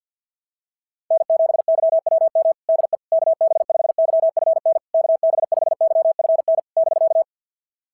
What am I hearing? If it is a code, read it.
Morse code, heard as a keyed tone, about 47 wpm, 660 Hz. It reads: N7YWM DE KB5YFN KB5YFN <BK>
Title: SOTA QSO Practice 50wpm